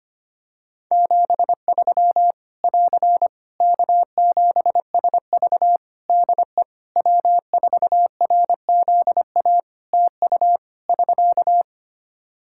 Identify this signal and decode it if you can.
Morse code, heard as a keyed tone, about 25 wpm, 700 Hz. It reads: 73 <AR> K7SV DE W4RZA TU <SK>
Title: Practice QSO - Exchange 4 - Ending - 25wpm